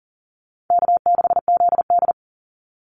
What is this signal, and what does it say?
Morse code, heard as a keyed tone, about 40 wpm, 700 Hz. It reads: K6ZD
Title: Easy Call Signs Repeated 2x - 40wpm